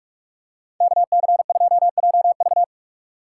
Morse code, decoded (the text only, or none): KC2JV